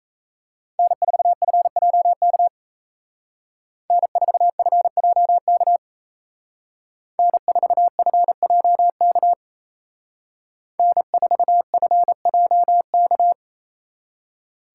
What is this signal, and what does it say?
Morse code, heard as a keyed tone, about 36 wpm, 700 Hz. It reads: N4FJK N4FJK N4FJK N4FJK